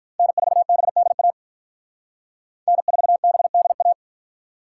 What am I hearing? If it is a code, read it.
Morse code, heard as a keyed tone, about 46 wpm, 700 Hz. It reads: N4BDA N4BDA